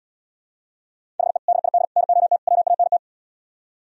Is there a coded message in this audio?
Yes: Morse code from 1.2 to 3.0 s, heard as a keyed tone, about 77 wpm, 700 Hz. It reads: HE DID NOT WANT